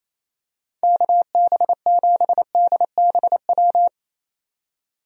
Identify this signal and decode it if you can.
Morse code, heard as a keyed tone, about 27 wpm, 700 Hz. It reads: KB7DBW
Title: USA Call Signs - 6 Characters - 28wpm